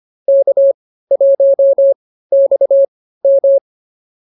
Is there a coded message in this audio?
Yes: Morse code from 0.3 to 3.6 s, heard as a keyed tone, about 24 wpm, spaced as for 15 wpm, 550 Hz. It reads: K1XM